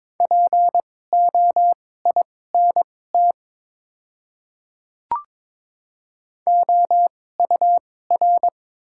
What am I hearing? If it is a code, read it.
Morse code, heard as a keyed tone, about 22 wpm, spaced as for 16 wpm, 700 Hz. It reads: POINT OUR
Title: 2x Character Spacing - Top 200 Words - 22wpm